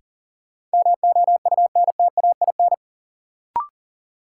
Morse code, heard as a keyed tone, about 39 wpm, 700 Hz. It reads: MOUNTAIN
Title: Sentences from Top 1000 Words - Review Part 1 - New Word Drill - 40wpm